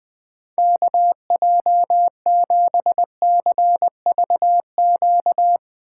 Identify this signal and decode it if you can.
Morse code, heard as a keyed tone, about 20 wpm, 700 Hz. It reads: KJ7CVQ